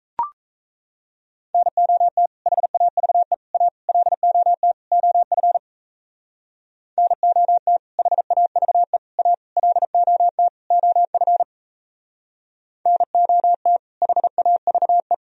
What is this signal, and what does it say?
Morse code, heard as a keyed tone, about 35 wpm, 700 Hz. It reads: NOT HAVE A LOT OF NOT HAVE A LOT OF NOT HAVE
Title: Speed-Racing - Sets of 5 Words 28wpm